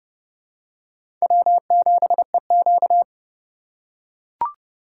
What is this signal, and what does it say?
Morse code, heard as a keyed tone, about 31 wpm, 700 Hz. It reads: W7EQ
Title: USA Call Signs - 2x Character Spacing - 30wpm